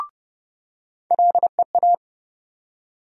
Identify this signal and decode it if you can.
Morse code, heard as a keyed tone, about 29 wpm, 700 Hz. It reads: LEA
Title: USA Call Sign Postfixes - 3 Letters 30wpm